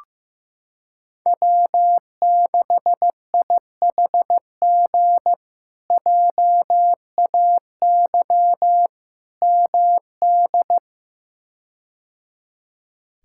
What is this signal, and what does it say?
Morse code, heard as a keyed tone, about 16 wpm, 700 Hz. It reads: W6IHG JAY MD
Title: K1USN SST QSO Practice - 15wpm